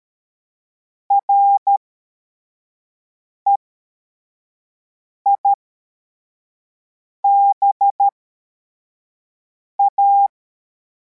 Morse code, heard as a keyed tone, about 13 wpm, spaced as for 4 wpm, 800 Hz. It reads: REIBA
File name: formation / cw / Text61